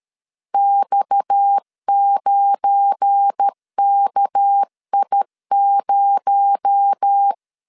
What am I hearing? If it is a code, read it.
Morse code, heard as a keyed tone, about 13 wpm, 800 Hz. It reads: X9KI0
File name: LEZIONI / 7 GF9Z / 7 GF9 12wpm 2319